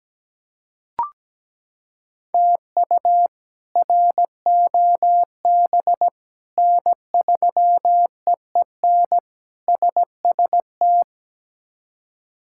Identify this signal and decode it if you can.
Morse code, heard as a keyed tone, about 17 wpm, 700 Hz. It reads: TU ROB N3EEN SST
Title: K1USN SST QSO Practice - 17wpm